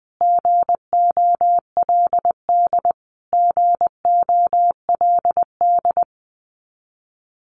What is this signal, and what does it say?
Morse code, heard as a keyed tone, about 19 wpm, 700 Hz. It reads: GOLD GOLD
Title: Top 500 Words Repeated 20wpm